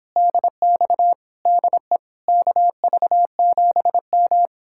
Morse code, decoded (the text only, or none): DX DE KV7M